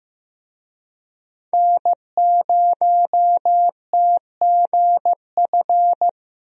Morse code, heard as a keyed tone, about 15 wpm, 700 Hz. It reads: N0TGF
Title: USA Call Signs - 5 Characters - 15wpm